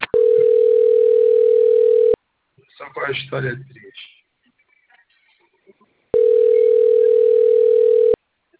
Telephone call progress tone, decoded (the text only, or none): ringback tone